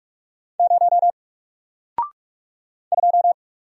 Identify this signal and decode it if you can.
Morse code, heard as a keyed tone, about 45 wpm, 700 Hz. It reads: 0 2
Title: Single Character (Letter:Number:Punctuation:Prosign) Alt 45wpm